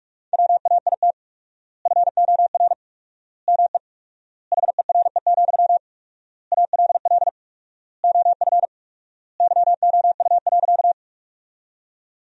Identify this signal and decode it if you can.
Morse code, heard as a keyed tone, about 46 wpm, 700 Hz. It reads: WAIT FOR ME HERE, ALL OF YOU.
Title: 4x Word-Spacing - Sentences from Top 500 Words - 45wpm